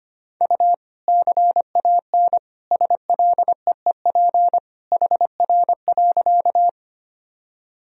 Morse code, heard as a keyed tone, about 25 wpm, 700 Hz. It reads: U CAN SLEEP HR.
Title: Sentences from Top 500 Words and Abbreviations 25wpm